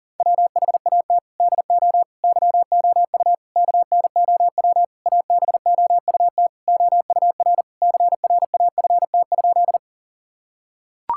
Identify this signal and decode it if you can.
Morse code, heard as a keyed tone, about 37 wpm, 700 Hz. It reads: WHAT DO YOU KNOW ABOUT OUR CRAFT?
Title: Sentences from Top 1000 Words - Review Part 6 - 40wpm